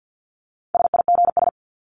Morse code, heard as a keyed tone, about 50 wpm, 700 Hz. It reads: SIGS
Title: Abbreviations 50wpm